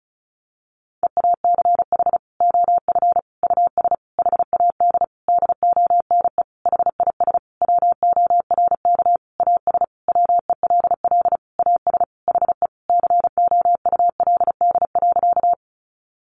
Morse code, heard as a keyed tone, about 34 wpm, 700 Hz. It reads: EACH OF US HAD DONE HIS WORK AS WELL AS HE COULD.
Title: Sentences from Top 500 Words 35wpm